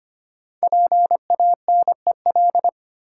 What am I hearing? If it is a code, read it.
Morse code, heard as a keyed tone, about 25 wpm, 700 Hz. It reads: PANEL
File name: 5 Letter Words - First 3 Letters Given - 25wpm